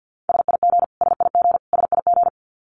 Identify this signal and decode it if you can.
Morse code, heard as a keyed tone, about 50 wpm, 700 Hz. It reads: SIG SIG SIG
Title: Abbreviations Repeated 3x - 50wpm